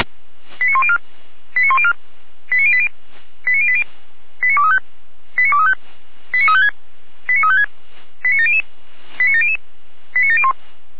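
Five-tone selective call (ZVEI), heard as quick runs of five tones, tones of about 70 ms. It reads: 89104 89104 89080 89080 89235 89235 89356 89356 89700 89700 89701
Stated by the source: ZVEI five-tone selective call on 455 received by SDR.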